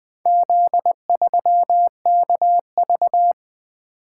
Morse code, heard as a keyed tone, about 20 wpm, 700 Hz. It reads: Z3KV
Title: Four Letter-Number Combinations - 20wpm